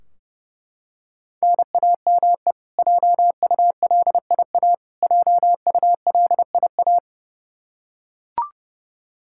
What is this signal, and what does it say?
Morse code, heard as a keyed tone, about 30 wpm, 700 Hz. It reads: NAME JULIA JULIA
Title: Practice QSO - Exchange 1 - Beginning - 30wpm